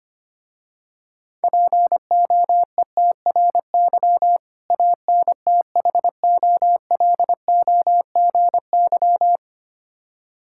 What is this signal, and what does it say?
Morse code, heard as a keyed tone, about 25 wpm, 700 Hz. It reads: POETRY ANTHOLOGY
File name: practice / 25wpm / Difficult Words with Hints - 25wpm